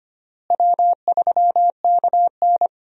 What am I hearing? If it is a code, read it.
Morse code, heard as a keyed tone, about 25 wpm, 700 Hz. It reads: W3KN
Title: USA Call Signs - 4 Characters - Repeated 2x - 25wpm